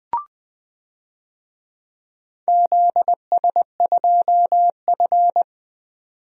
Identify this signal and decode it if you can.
Morse code, heard as a keyed tone, about 20 wpm, 700 Hz. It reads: ZS2F
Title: DXCC-100 Call Signs - 4 Characters - 20wpm